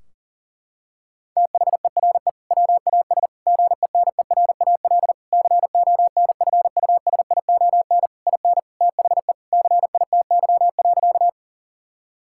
Morse code, heard as a keyed tone, about 37 wpm, 700 Hz. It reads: THERE WAS GENERAL CONFUSION IN THE CITY.
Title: Sentences from Top 1600 Words 40wpm